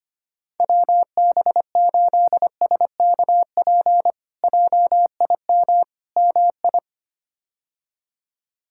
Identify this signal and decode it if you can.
Morse code, heard as a keyed tone, about 25 wpm, 700 Hz. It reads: WB8SKP JIM MI
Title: K1USN SST QSO Practice - 25wpm